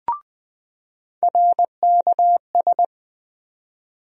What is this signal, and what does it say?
Morse code, heard as a keyed tone, about 21 wpm, 700 Hz. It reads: RKS
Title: USA Call Sign Postfixes - 3 Letters 20wpm